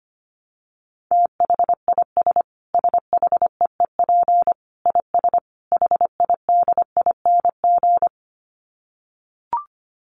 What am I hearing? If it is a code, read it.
Morse code, heard as a keyed tone, about 25 wpm, 700 Hz. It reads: THIS SHEEP IS HIDING
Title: Dit Challenge 25wpm